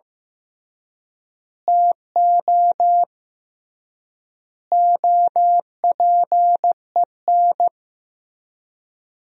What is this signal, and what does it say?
Morse code, heard as a keyed tone, about 15 wpm, 700 Hz. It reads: TO OPEN
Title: 3x Word-Spacing - Sentences from Top 300 Words - 15wpm